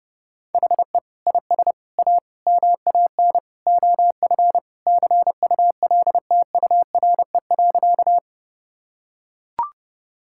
Morse code, heard as a keyed tone, about 30 wpm, 700 Hz. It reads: HE IS A MAN OF CULTURE.
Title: Sentences from Top 1000 Words - Review Part 6 - 30wpm